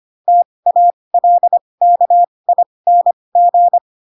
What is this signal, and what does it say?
Morse code, heard as a keyed tone, about 25 wpm, spaced as for 20 wpm, 700 Hz. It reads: TALKING
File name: Pinocchio / Pinocchio ch4 15 25 ik1hge